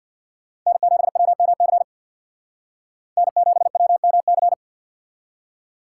Morse code, heard as a keyed tone, about 61 wpm, 700 Hz. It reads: N7WMC N7WMC
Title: Speed Racing - USA 1x3 Call Signs 50wpm